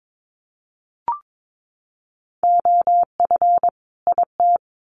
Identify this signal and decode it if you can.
Morse code, heard as a keyed tone, about 22 wpm, 700 Hz. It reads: OF IT
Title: Sets of 2 Words 22wpm